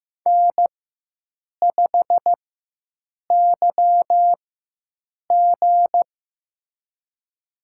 Morse code, heard as a keyed tone, about 15 wpm, spaced as for 7 wpm, 700 Hz. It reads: N5YG
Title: USA Call Signs - 4 Characters - 4x Character Spacing - 15wpm